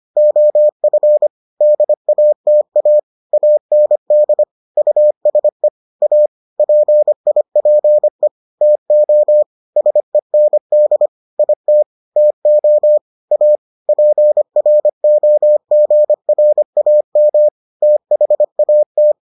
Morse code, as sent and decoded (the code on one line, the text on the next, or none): --- ..-. / -.. .- - .- / .- -. -.. / ..- ... / .- / .--. .. .--. / - --- / ... -. -.. / .. - / - --- / .- / .--. .-. --- --. .-. .- -- / - .... .- -
OF DATA AND USE A PIPE TO SEND IT TO A PROGRAM THAT